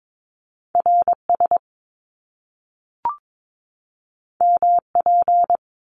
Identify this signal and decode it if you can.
Morse code, heard as a keyed tone, about 22 wpm, 700 Hz. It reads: RS MP
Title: Top Two Letter Combinations in Words 22wpm